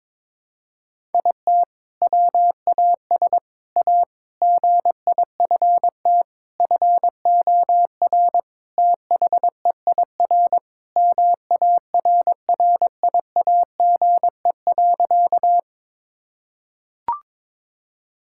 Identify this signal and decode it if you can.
Morse code, heard as a keyed tone, about 22 wpm, 700 Hz. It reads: IT WAS A GIFT FOR THEIR MARRIAGE.
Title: Sentences from Top 1000 Words - Review Part 6 - 22wpm